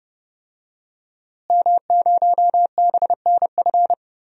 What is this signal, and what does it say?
Morse code, heard as a keyed tone, about 30 wpm, 700 Hz. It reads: M0BNF
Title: DXCC-100 Call Signs - 5 Characters - Repeated 2x - 30wpm